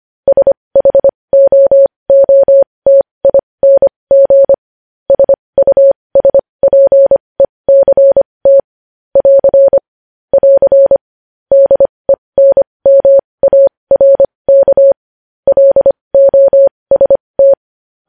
Morse code, heard as a keyed tone, about 25 wpm, spaced as for 20 wpm, 550 Hz. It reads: SHOOTING SUSPECT <AR> <AR> DENMARK LOST
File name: podcast / News headlines-1x-25x20